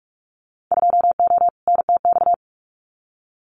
Morse code, heard as a keyed tone, about 44 wpm, 700 Hz. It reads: 2O NTX
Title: Winter Field Day Exchanges 45wpm